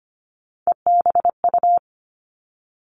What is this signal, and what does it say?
Morse code, heard as a keyed tone, about 25 wpm, 700 Hz. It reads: EBU
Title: Three Challenging Letter-Number Combinations 25wpm